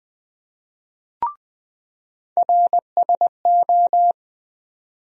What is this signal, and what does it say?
Morse code, heard as a keyed tone, about 20 wpm, 700 Hz. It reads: RSO
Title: USA Call Sign Postfixes - 3 Letters 20wpm